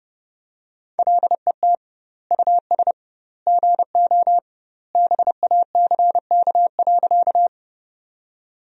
Morse code, heard as a keyed tone, about 30 wpm, 700 Hz. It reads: LET US GO BACK.
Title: 2x Word-Spacing - Sentences from Top 300 Words - 30wpm